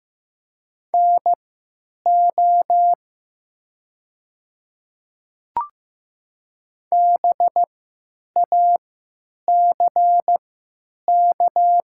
Morse code, heard as a keyed tone, about 15 wpm, spaced as for 9 wpm, 700 Hz. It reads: NO BACK
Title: 3x Character Spacing - Top 300 Words - 15wpm